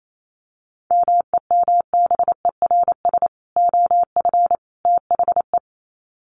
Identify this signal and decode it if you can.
Morse code, heard as a keyed tone, about 28 wpm, 700 Hz. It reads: MEMBERS OF THE
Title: Sets of 3 Words 28wpm